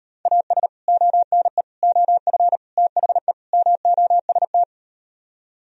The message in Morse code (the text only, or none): AS ONE OF THE MOST